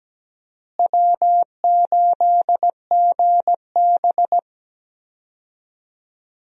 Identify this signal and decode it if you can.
Morse code, heard as a keyed tone, about 17 wpm, 700 Hz. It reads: W8GB
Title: USA Call Signs - 4 Characters - Repeated 2x - 17wpm